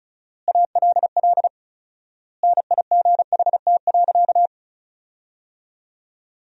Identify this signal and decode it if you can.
Morse code, heard as a keyed tone, about 36 wpm, 700 Hz. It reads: ALL NIGHT.
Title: 4x Word-Spacing - Sentences from Top 500 Words - 35wpm